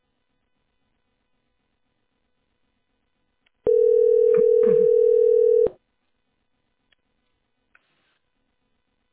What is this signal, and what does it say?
Ringback tone (North America), 440 + 480 Hz.
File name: 8434ad87-f5d8-4a85-bdf2